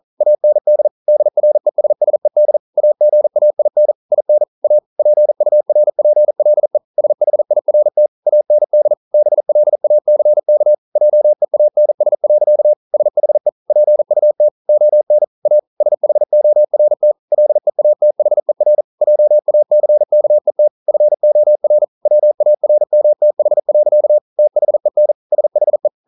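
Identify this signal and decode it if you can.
Morse code, heard as a keyed tone, about 42 wpm, 600 Hz. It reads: AND DRESSED AGAIN IN A PURPLE SHIRT AND BLACK JEANS. SHE PUT ON A SHORT LEATHER JACKET FOR WARMTH. THEN SHE
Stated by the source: Morse code practice at 41 wpm, from CW med Gnister.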